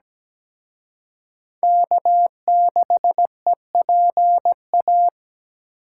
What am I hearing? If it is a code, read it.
Morse code, heard as a keyed tone, about 20 wpm, 700 Hz. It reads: K6EPA
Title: Speed Racing - USA 1x3 Call Signs 17wpm